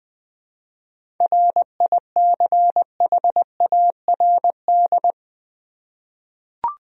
Morse code, heard as a keyed tone, about 20 wpm, 700 Hz. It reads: RICHARD
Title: USA Common Names - 20wpm